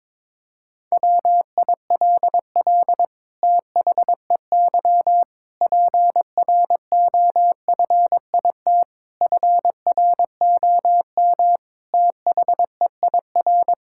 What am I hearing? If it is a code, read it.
Morse code, heard as a keyed tone, about 22 wpm, 700 Hz. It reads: WILL THEY PROFIT FROM THEIR
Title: Sentences from Top 1000 Words - Review Part 6 - 22wpm